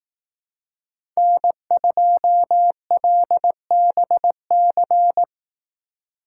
Morse code, heard as a keyed tone, about 18 wpm, 700 Hz. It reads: N2LBC
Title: Speed Racing - USA 1x3 Call Signs 15wpm